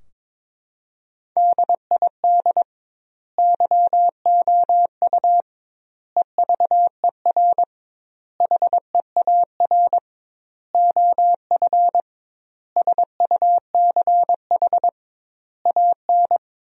Morse code, heard as a keyed tone, about 22 wpm, 700 Hz. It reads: DID YOU EVER HEAR OF SUCH AN
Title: 2x Word-Spacing - Sentences from Top 400 Words - 22wpm